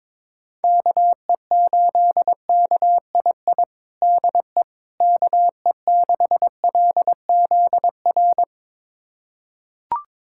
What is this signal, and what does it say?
Morse code, heard as a keyed tone, about 22 wpm, 700 Hz. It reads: KE8KII DE KE6LZR
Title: USA Callsign DE Callsign - 6 Characters - 22wpm